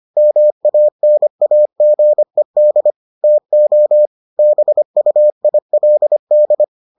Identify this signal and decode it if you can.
Morse code, heard as a keyed tone, about 25 wpm, 600 Hz. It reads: MANAGED TO BUILD